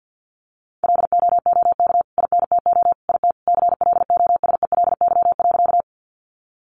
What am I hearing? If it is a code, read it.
Morse code, heard as a keyed tone, about 50 wpm, 700 Hz. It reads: LOOK INTO IT CLOSELY.
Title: Sentences from Top 1400 Words 50wpm